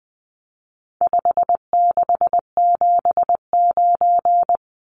Morse code, heard as a keyed tone, about 20 wpm, 700 Hz. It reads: 5679
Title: Numbers - 4-Digit 20wpm